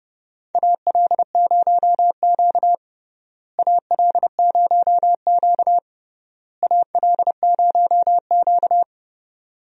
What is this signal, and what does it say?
Morse code, heard as a keyed tone, about 32 wpm, 700 Hz. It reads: AL0Q AL0Q AL0Q
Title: USA Call Signs - 4 Characters - Repeated 3x - 30wpm